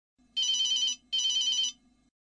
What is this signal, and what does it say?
There is no code.